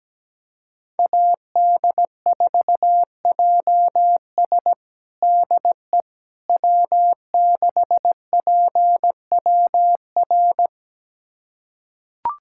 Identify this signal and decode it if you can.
Morse code, heard as a keyed tone, about 17 wpm, 700 Hz. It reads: AD4JS DE W6PWR
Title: USA Callsign DE Callsign - 5 Characters - 17wpm